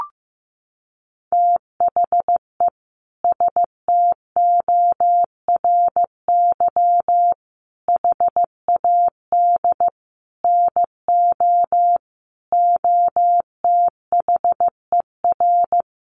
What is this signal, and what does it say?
Morse code, heard as a keyed tone, about 15 wpm, 700 Hz. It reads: THE STORY HAD NO OTHER
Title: Sentences from Top 1200 Words 15wpm